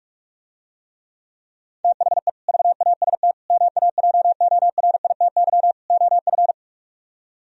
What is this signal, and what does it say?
Morse code, heard as a keyed tone, about 48 wpm, 700 Hz. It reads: THE VAST MAJORITY OF